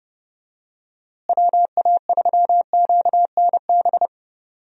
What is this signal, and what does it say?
Morse code, heard as a keyed tone, about 31 wpm, 700 Hz. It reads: WA3QNB
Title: USA Call Signs - 6 Characters - Repeated 2x - 30wpm